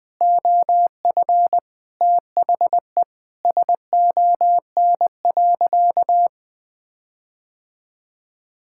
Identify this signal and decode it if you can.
Morse code, heard as a keyed tone, about 20 wpm, 700 Hz. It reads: OF THE SON.
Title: Sentences with 9-Word Phrases - Spoken Beginning - 20wpm